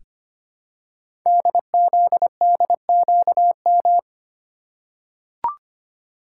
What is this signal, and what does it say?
Morse code, heard as a keyed tone, about 25 wpm, 700 Hz. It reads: DZDQM
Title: Code Groups - 5 Letters - 25wpm